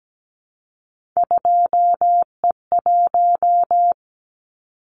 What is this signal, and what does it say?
Morse code, heard as a keyed tone, about 17 wpm, 700 Hz. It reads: 2E1
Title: Callsign Prefixes for Top 25 DXCC - Repeated 3x - 17wpm